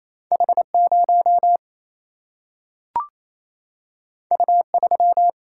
Morse code, heard as a keyed tone, about 28 wpm, 700 Hz. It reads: H0 U3
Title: Single-Letter Single-Number 28wpm